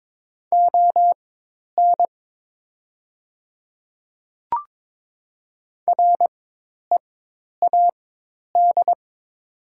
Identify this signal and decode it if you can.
Morse code, heard as a keyed tone, about 22 wpm, spaced as for 10 wpm, 700 Hz. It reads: ON READ
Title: 4x Character Spacing - Top 200 Words - 22wpm